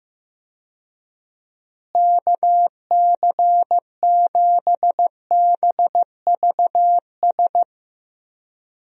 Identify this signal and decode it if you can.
Morse code, heard as a keyed tone, about 15 wpm, 700 Hz. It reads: KC7BVS
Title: USA Call Signs - 6 Characters - Repeated 2x - 15wpm